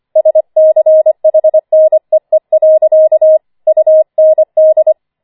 Morse code, heard as a keyed tone, about 25 wpm, 630 Hz. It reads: SCHNEE. UND